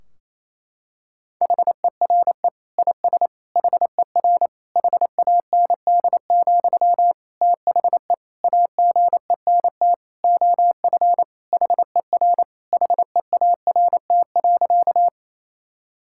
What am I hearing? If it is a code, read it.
Morse code, heard as a keyed tone, about 28 wpm, 700 Hz. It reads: HERE IS HER HAND, THE AGENT OF HER HEART.
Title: Sentences from Top 1600 Words 28wpm